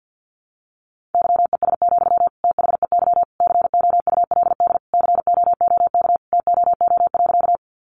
Morse code, heard as a keyed tone, about 50 wpm, 700 Hz. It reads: YES, THEY COULD COOK TOO.
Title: Sentences from Top 700 Words 50wpm